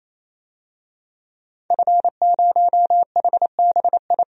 Morse code, heard as a keyed tone, about 28 wpm, 700 Hz. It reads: F0HBI
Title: DXCC-100 Call Signs - 5 Characters - 28wpm